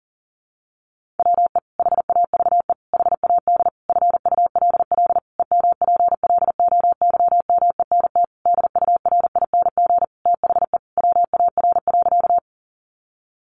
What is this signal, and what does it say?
Morse code, heard as a keyed tone, about 40 wpm, 700 Hz. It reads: WE HAVE HAD FULL EMPLOYMENT DURING THE WAR.
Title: Sentences from Top 1500 Words 40wpm